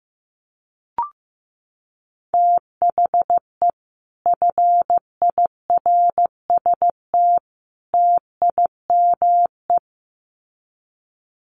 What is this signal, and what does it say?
Morse code, heard as a keyed tone, about 15 wpm, 700 Hz. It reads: THE FIRST TIME
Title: Sets of 3 Words 15wpm